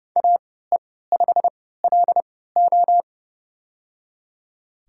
Morse code, heard as a keyed tone, about 30 wpm, spaced as for 17 wpm, 700 Hz. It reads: AE5LO
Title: USA Call Signs - 3x Character Spacing - 30wpm